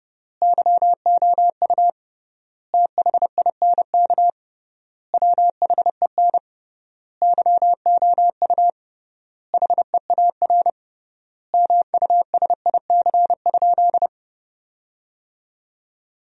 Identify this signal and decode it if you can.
Morse code, heard as a keyed tone, about 30 wpm, 700 Hz. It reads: YOU THINK WHEN YOU HEAR MUSIC?
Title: 3x Word-Spacing - Sentences from Top 300 Words - 30wpm